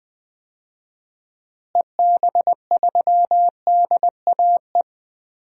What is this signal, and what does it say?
Morse code, heard as a keyed tone, about 20 wpm, 700 Hz. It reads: EB3DAE
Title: DXCC-100 Call Signs - 6 Characters - Repeated 3x - 20wpm